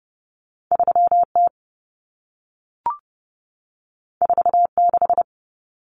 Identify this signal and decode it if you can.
Morse code, heard as a keyed tone, about 30 wpm, 700 Hz. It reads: 3T 46
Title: Common Cut Numbers - 2-Digit 30wpm